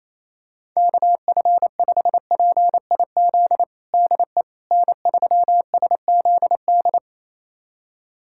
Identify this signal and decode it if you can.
Morse code, heard as a keyed tone, about 28 wpm, 700 Hz. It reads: KF5PIZ DE N3SZD
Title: USA Callsign DE Callsign 28wpm